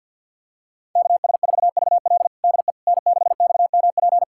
Morse code, heard as a keyed tone, about 45 wpm, 700 Hz. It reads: KS4VL DE N6XMP
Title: USA Callsign DE Callsign - 5 Characters - 50wpm